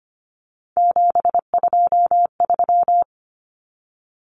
Morse code, heard as a keyed tone, about 25 wpm, 700 Hz. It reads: 723
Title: Numbers - 3-Digit 25wpm